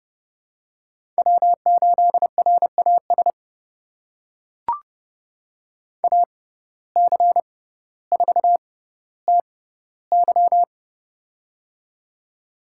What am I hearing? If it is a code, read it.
Morse code, heard as a keyed tone, about 30 wpm, spaced as for 15 wpm, 700 Hz. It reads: W8RAS AC4TY
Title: USA Call Signs - 5 Characters - 6x Character Spacing - 30wpm